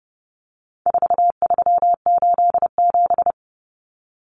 Morse code, heard as a keyed tone, about 30 wpm, 700 Hz. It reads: 4387
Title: Numbers - 4-Digit 30wpm